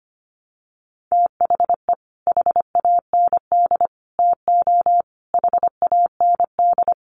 Morse code, heard as a keyed tone, about 25 wpm, 700 Hz. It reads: THE HAND TO HAND
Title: Sentences from Top 1300 Words 25wpm